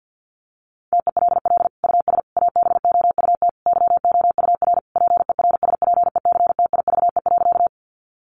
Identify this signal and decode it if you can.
Morse code, heard as a keyed tone, about 50 wpm, 700 Hz. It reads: TELL US ABOUT YOUR PERSPECTIVE.
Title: Sentences from Top 1200 Words 50wpm